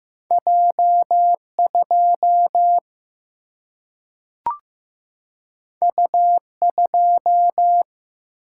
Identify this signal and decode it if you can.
Morse code, heard as a keyed tone, about 15 wpm, 700 Hz. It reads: J2 U2
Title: Single-Letter Single-Number 15wpm